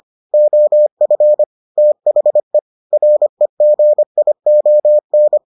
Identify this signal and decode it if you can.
Morse code, heard as a keyed tone, about 25 wpm, 600 Hz. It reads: OF THE REGION